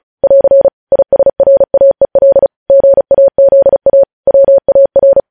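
Morse code, heard as a keyed tone, about 36 wpm, 550 Hz. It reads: <AR> ISRAEL GAZA WAR
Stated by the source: Morse code news headlines at 35.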